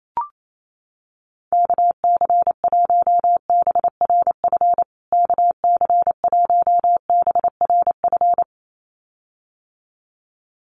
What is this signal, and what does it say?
Morse code, heard as a keyed tone, about 28 wpm, 700 Hz. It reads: KC1BRF KC1BRF
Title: USA Call Signs Repeated 2x 28wpm